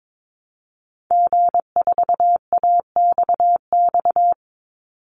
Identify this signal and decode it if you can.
Morse code, heard as a keyed tone, about 22 wpm, 700 Hz. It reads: G4AXX
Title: Call Signs - 22wpm